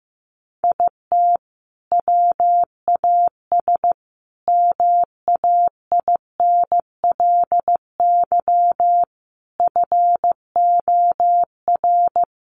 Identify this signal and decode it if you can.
Morse code, heard as a keyed tone, about 15 wpm, 700 Hz. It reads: IT WAS MAINLY FOR
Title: Sentences from Top 1400 Words 15wpm